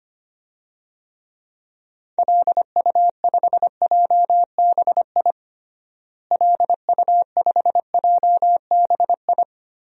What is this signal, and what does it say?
Morse code, heard as a keyed tone, about 25 wpm, 700 Hz. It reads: LU5JBI LU5JBI
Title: DXCC-100 Call Signs - 6 Characters - Repeated 3x - 25wpm